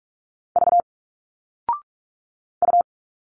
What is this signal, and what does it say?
Morse code, heard as a keyed tone, about 46 wpm, 700 Hz. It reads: V U
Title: Single Letters 45wpm